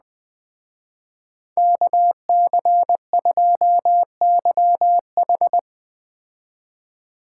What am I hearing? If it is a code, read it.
Morse code, heard as a keyed tone, about 21 wpm, 700 Hz. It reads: KC2YH